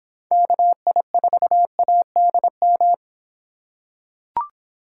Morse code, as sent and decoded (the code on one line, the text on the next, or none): -.- .. ....- .- -.. --
KI4ADM